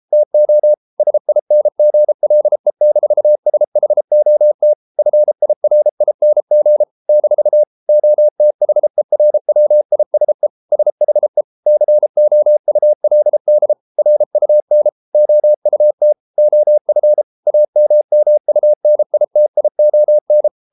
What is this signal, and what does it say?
Morse code, heard as a keyed tone, about 33 wpm, 600 Hz. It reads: TO SINGLE-SHOT FIRING - OTHERWISE SHE COULD RUN OUT OF AMMUNITION